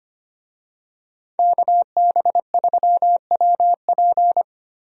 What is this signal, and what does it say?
Morse code, heard as a keyed tone, about 25 wpm, 700 Hz. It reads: KB3WP
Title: USA Call Signs - 5 Characters - Repeated 2x - 25wpm